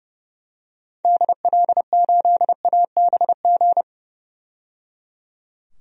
Morse code, heard as a keyed tone, about 30 wpm, 700 Hz. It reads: DL8ABG
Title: DXCC-100 Call Signs - 6 Characters - 30wpm